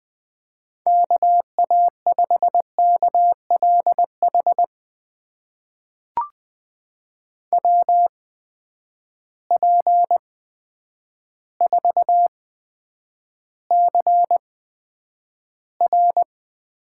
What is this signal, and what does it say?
Morse code, heard as a keyed tone, about 20 wpm, spaced as for 9 wpm, 700 Hz. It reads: KA5KLH WP4CR
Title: USA Call Signs - 6 Characters - 8x Character Spacing - 20wpm